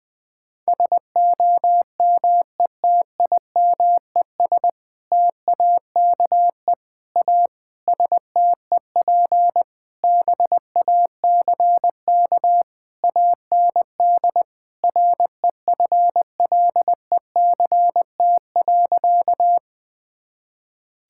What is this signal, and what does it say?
Morse code, heard as a keyed tone, about 19 wpm, 700 Hz. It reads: SOMETIMES TAKE A STEP BACK AND REFLECT.
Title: Sentences with 6-Word Phrases - Spoken Beginning - 20wpm